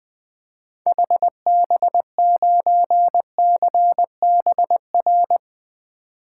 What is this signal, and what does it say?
Morse code, heard as a keyed tone, about 21 wpm, 700 Hz. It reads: HB9CBR
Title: 2023 SOTA Activator Call Signs - 20wpm